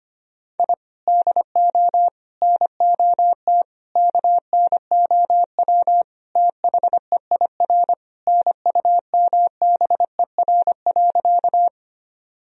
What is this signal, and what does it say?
Morse code, heard as a keyed tone, about 25 wpm, 700 Hz. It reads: I DO NOT KNOW THEIR NUMBER.
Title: Sentences from Top 1000 Words - Review Part 1 - 25wpm